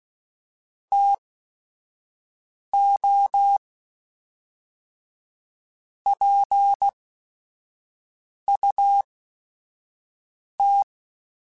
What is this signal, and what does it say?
Morse code, heard as a keyed tone, about 16 wpm, spaced as for 5 wpm, 800 Hz. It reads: TO PUT